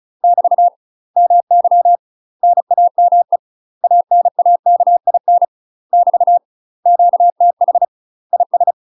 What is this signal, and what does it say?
Morse code, heard as a keyed tone, about 34 wpm, 700 Hz. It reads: <BT> MY NAME ANAKIN <BT> QTH IS
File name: UK MASTER QSOs TRAINING / 35WPM / UK MASTER QSOs 0036